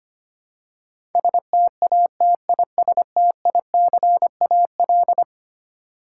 Morse code, heard as a keyed tone, about 25 wpm, 700 Hz. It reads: STATISTICAL
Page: morse-code-ninja-practice-25wpm